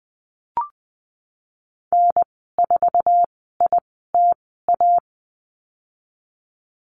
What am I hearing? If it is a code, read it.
Morse code, heard as a keyed tone, about 20 wpm, spaced as for 15 wpm, 700 Hz. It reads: N4ITA
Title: 076 Lesson 13g - Letter L - Call Signs 20wpm